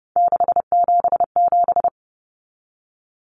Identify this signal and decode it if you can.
Morse code, heard as a keyed tone, about 30 wpm, 700 Hz. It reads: 677
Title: Numbers - 3-Digit 30wpm